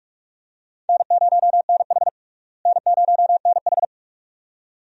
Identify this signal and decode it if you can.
Morse code, heard as a keyed tone, about 46 wpm, 700 Hz. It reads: N0NH N0NH
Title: USA Call Signs - 4 Characters - Repeated 2x - 45wpm